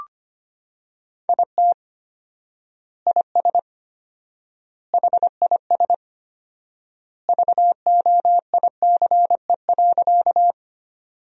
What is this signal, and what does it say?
Morse code, heard as a keyed tone, about 25 wpm, 700 Hz. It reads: IT IS HIS VOICE.
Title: 4x Word-Spacing - Sentences from Top 500 Words - 25wpm